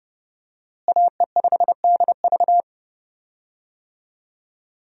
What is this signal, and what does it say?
Morse code, heard as a keyed tone, about 31 wpm, 700 Hz. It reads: AE5DV
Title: USA Call Signs - 5 Characters - Repeated 2x - 30wpm